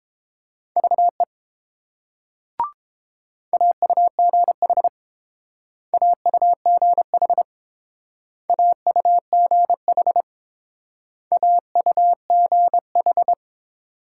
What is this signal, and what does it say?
Morse code, heard as a keyed tone, about 29 wpm, 700 Hz. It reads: VE AUGH AUGH AUGH AUGH